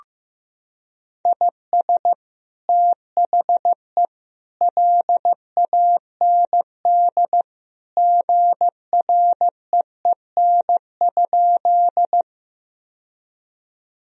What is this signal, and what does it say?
Morse code, heard as a keyed tone, about 15 wpm, 700 Hz. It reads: IS THE LAND GREEN?
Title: Sentences from Top 1000 Words - Review Part 2 - 15wpm